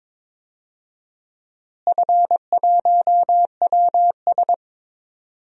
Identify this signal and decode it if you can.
Morse code, heard as a keyed tone, about 21 wpm, 700 Hz. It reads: F1WS
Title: DXCC-100 Call Signs - 4 Characters - Repeated 3x - 22wpm